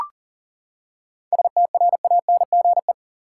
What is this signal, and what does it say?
Morse code, heard as a keyed tone, about 40 wpm, 700 Hz. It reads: STRANGE
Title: Seven Letter Words - 40wpm